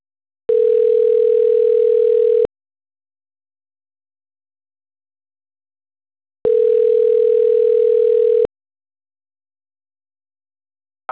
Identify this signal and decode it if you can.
Ringback tone (North America), 440 + 480 Hz.